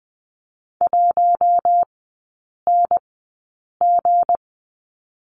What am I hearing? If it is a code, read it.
Morse code, heard as a keyed tone, about 20 wpm, spaced as for 8 wpm, 690 Hz. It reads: 1NG
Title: 159 Lesson 26e - Letter g - Three Characters 20wpm